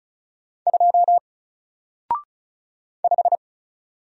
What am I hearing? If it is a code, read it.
Morse code, heard as a keyed tone, about 34 wpm, 700 Hz. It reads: 2 5
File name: Single-Letter-Number-35wpm